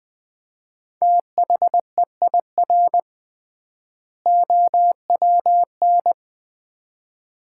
Morse code, heard as a keyed tone, about 20 wpm, 700 Hz. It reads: THEIR OWN
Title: 3x Word-Spacing - Sets of 2 Words - 20wpm